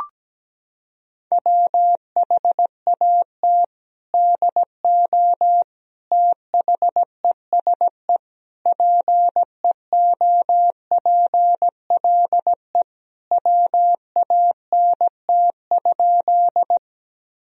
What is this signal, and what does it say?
Morse code, heard as a keyed tone, about 17 wpm, 700 Hz. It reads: WHAT DO THESE PEOPLE WANT?
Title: Sentences from Top 1000 Words - Review Part 1 - 17wpm